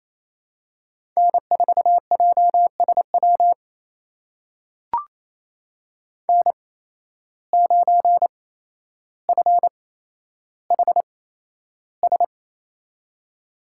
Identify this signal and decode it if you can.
Morse code, heard as a keyed tone, about 28 wpm, spaced as for 12 wpm, 700 Hz. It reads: N4JSW N9FHS
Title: USA Call Signs - 5 Characters - 8x Character Spacing - 28wpm